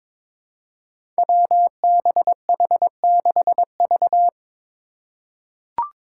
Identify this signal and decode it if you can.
Morse code, heard as a keyed tone, about 22 wpm, 700 Hz. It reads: WBH6V
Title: Five Challenging Letter-Number Combinations 22wpm